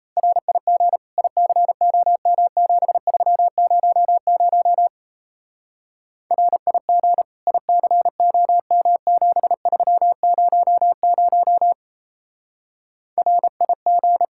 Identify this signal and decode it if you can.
Morse code, heard as a keyed tone, about 35 wpm, 700 Hz. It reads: RIG ICOM7300 RIG ICOM7300 RIG